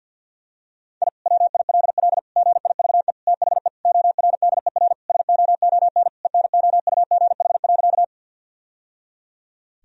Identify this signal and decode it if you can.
Morse code, heard as a keyed tone, about 49 wpm, 700 Hz. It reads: I WILL GIVE THE ORDER SOON ENOUGH.
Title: Sentences from Top 1000 Words - Review Part 2 - 50wpm